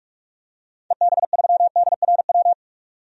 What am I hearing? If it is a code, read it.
Morse code, heard as a keyed tone, about 48 wpm, 700 Hz. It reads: EB3DRW
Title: DXCC-100 Call Signs - 6 Characters - 45wpm